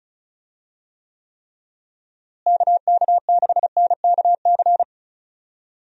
Morse code, heard as a keyed tone, about 36 wpm, 700 Hz. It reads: KK6NKC
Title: USA Call Signs - 6 Characters - Repeated 3x - 35wpm